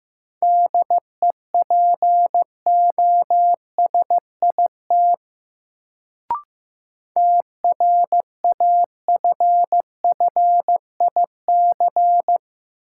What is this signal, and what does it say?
Morse code, heard as a keyed tone, about 15 wpm, 700 Hz. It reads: DEPOSIT TRAFFIC
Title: Sentences from Top 1300 Words - New Word Drill Alt - 15wpm